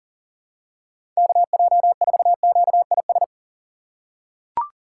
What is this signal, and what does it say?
Morse code, heard as a keyed tone, about 40 wpm, 700 Hz. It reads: KJ4QIS
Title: USA Call Signs - 6 Characters - 2x Character Spacing - 40wpm